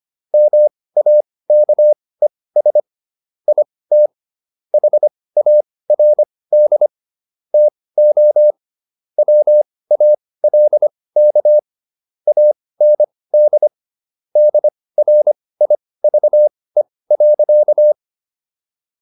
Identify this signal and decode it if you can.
Morse code, heard as a keyed tone, about 25 wpm, spaced as for 18 wpm, 600 Hz. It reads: MAKES IT HARD TO WALK AND DRIVE.